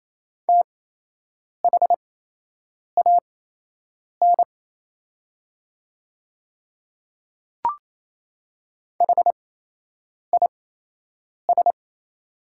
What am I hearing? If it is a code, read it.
Morse code, heard as a keyed tone, about 28 wpm, spaced as for 8 wpm, 700 Hz. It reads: THAN HIS
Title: 8x Character Spacing - Top 100 Words - 28wpm